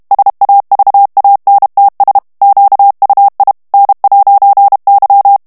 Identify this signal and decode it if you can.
Morse code, heard as a keyed tone, about 32 wpm, 800 Hz. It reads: SAVANTS QUI N'Y
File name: CW / FICHIERS-MP3 / FRANCAIS TEXTE CLAIR 2 VITESSE 30 MOTS MINUTES